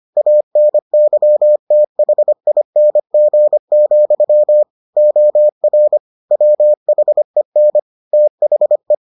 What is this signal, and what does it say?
Morse code, heard as a keyed tone, about 25 wpm, 600 Hz. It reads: ANYTHING, OR WHEN THE